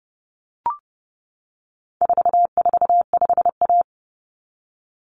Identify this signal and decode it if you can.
Morse code, heard as a keyed tone, about 28 wpm, 700 Hz. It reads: 445A
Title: Common Cut Numbers - 4-Digit 30wpm